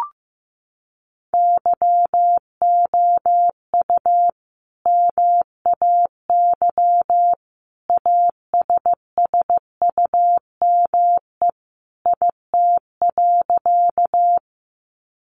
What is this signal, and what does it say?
Morse code, heard as a keyed tone, about 15 wpm, 700 Hz. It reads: YOU MAY ASSUME IT.
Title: Sentences from Top 1500 Words 15wpm